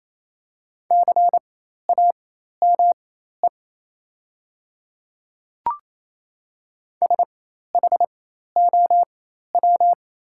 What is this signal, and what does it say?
Morse code, heard as a keyed tone, about 28 wpm, spaced as for 13 wpm, 700 Hz. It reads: CAME SHOW
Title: 4x Character Spacing - Top 200 Words - 28wpm